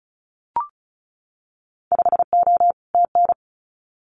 Morse code, heard as a keyed tone, about 36 wpm, 700 Hz. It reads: 5O TN
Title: Winter Field Day Exchanges Encore 35wpm